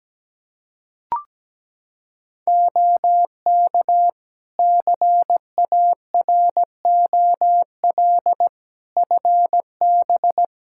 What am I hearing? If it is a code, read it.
Morse code, heard as a keyed tone, about 17 wpm, 700 Hz. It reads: OK CAROL FB
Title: K1USN SST QSO Practice 17wpm